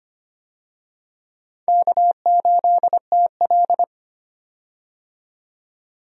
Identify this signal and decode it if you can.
Morse code, heard as a keyed tone, about 25 wpm, 700 Hz. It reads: K8TL
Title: USA Call Signs - 4 Characters - 25wpm